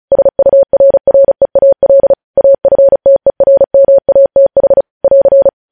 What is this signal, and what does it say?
Morse code, heard as a keyed tone, about 36 wpm, 550 Hz. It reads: SURREAL AFTERMATH <AR>
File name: podcast / News headlines-1x-35x35